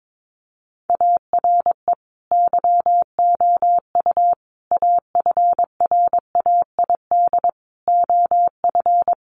Morse code, heard as a keyed tone, about 22 wpm, 700 Hz. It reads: ARE YOU AFRAID OF
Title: Sentences from Top 1500 Words 22wpm